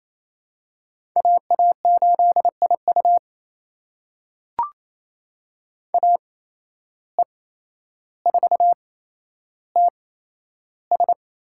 Morse code, heard as a keyed tone, about 27 wpm, spaced as for 12 wpm, 700 Hz. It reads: AA8IU AE4TS
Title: USA Call Signs - 5 Characters - 8x Character Spacing - 28wpm